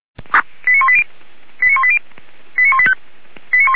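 Five-tone selective call (ZVEI), heard as quick runs of five tones, tones of about 70 ms. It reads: 89180 89180 89185
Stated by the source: ZVEI five-tone call received on SDR.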